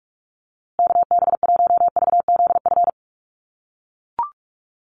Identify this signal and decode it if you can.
Morse code, heard as a keyed tone, about 44 wpm, 700 Hz. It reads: KB1VZF
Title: USA Call Signs Repeated 2x 45wpm